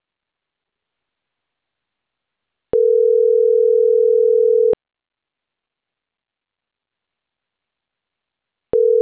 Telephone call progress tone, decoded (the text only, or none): ringback tone